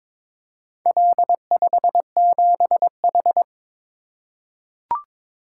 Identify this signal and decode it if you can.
Morse code, heard as a keyed tone, about 22 wpm, 700 Hz. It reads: L57H
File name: practice / 22wpm / Four Letter-Number Combinations - 22wpm